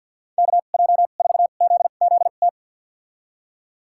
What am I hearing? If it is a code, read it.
Morse code, heard as a keyed tone, about 47 wpm, spaced as for 35 wpm, 700 Hz. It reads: KJ4ZZT
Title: USA Call Signs - 2x Character Spacing - 50wpm